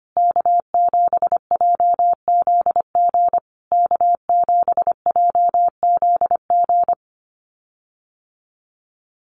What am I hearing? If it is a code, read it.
Morse code, heard as a keyed tone, about 25 wpm, 700 Hz. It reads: K7JZG K7JZG
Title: USA Call Signs Repeated 3x 25wpm